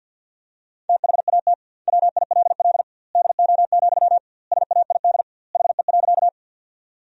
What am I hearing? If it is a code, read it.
Morse code, heard as a keyed tone, about 50 wpm, 700 Hz. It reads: THAT WILL DO, SAID HE.
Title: 2x Word-Spacing - Sentences from Top 100 Words - 50wpm